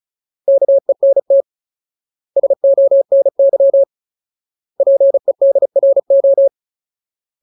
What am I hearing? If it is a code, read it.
Morse code, heard as a keyed tone, about 34 wpm, 550 Hz. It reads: KENT SONY PEDRO
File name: Ultimate Word List / 35WPM Ultimate